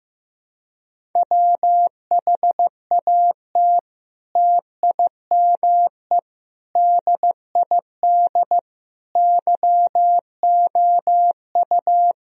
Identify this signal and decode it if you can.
Morse code, heard as a keyed tone, about 15 wpm, 700 Hz. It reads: WHAT TIME DID YOU
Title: Sentences from Top 1000 Words - Review Part 1 - 15wpm